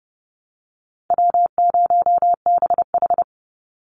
Morse code, heard as a keyed tone, about 30 wpm, 700 Hz. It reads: W0BH
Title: Easy Call Signs - 30wpm